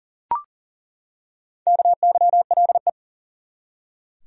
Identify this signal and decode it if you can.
Morse code, heard as a keyed tone, about 40 wpm, 700 Hz. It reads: KYLE
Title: USA Common Names - Max 4 Letters - 40wpm